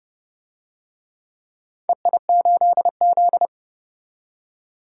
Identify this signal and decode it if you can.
Morse code, heard as a keyed tone, about 30 wpm, 700 Hz. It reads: EI8Z
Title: DXCC-100 Call Signs - 4 Characters - 30wpm